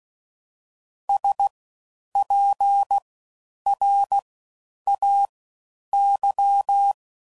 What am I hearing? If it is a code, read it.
Morse code, heard as a keyed tone, about 16 wpm, 800 Hz. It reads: S P R A Y
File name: Suffixes / Special INGs / Fast Ings